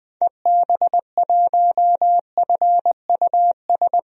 Morse code, heard as a keyed tone, about 20 wpm, 700 Hz. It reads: EB1FUS
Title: DXCC-100 Call Signs - 6 Characters - 20wpm